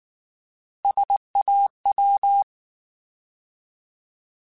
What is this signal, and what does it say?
Morse code, heard as a keyed tone, about 19 wpm, 800 Hz. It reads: SAW
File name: Alphabetical / Alphabetical mp3s / Ss @18